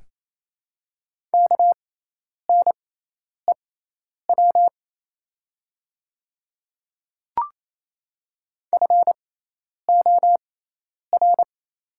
Morse code, heard as a keyed tone, about 27 wpm, spaced as for 10 wpm, 700 Hz. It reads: KNEW FOR
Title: 6x Character Spacing - Top 500 Words - 28wpm